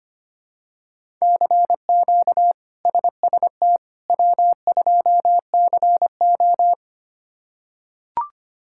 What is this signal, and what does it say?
Morse code, heard as a keyed tone, about 25 wpm, 700 Hz. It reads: CQ SST W2CO
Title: K1USN SST QSO Practice - 25wpm